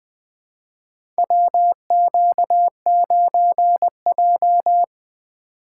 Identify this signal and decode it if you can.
Morse code, heard as a keyed tone, about 20 wpm, 700 Hz. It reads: WQ9J
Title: USA Call Signs - 4 Characters - 3x Character Spacing - 20wpm